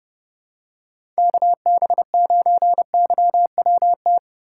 Morse code, heard as a keyed tone, about 30 wpm, 700 Hz. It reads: KB9YWT
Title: USA Call Signs - 6 Characters - 2x Character Spacing - 30wpm